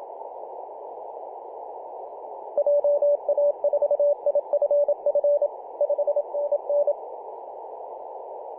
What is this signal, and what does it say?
Morse code, heard as a keyed tone, about 27 wpm, 580 Hz, with radio noise. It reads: JA4IFF 5NN
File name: PJ2/G3TXF 18.086MHz CW